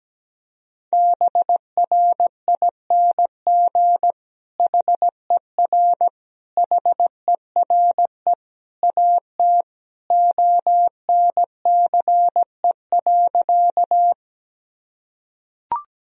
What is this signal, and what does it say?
Morse code, heard as a keyed tone, about 17 wpm, 700 Hz. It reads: BRING HER HERE AT ONCE.
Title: Sentences from Top 1000 Words - Review Part 3 - 17wpm